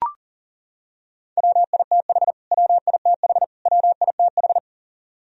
Morse code, heard as a keyed tone, about 40 wpm, 700 Hz. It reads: WITH WITH WITH
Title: Top 100 Words Repeated 40wpm